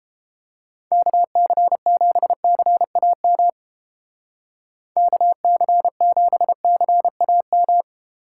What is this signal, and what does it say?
Morse code, heard as a keyed tone, about 30 wpm, 700 Hz. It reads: KC7CAM KC7CAM